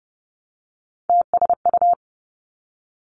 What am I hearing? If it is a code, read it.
Morse code, heard as a keyed tone, about 30 wpm, 700 Hz. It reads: TSU
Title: Three Challenging Letter-Number Combinations 30wpm